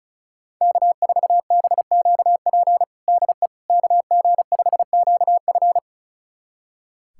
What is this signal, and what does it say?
Morse code, heard as a keyed tone, about 34 wpm, 700 Hz. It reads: K4BQP DE KG5QF
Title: USA Callsign DE Callsign 35wpm